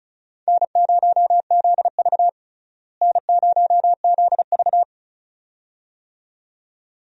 Morse code, heard as a keyed tone, about 34 wpm, 700 Hz. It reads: N0ZV N0ZV
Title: USA Call Signs - 4 Characters - Repeated 2x - 35wpm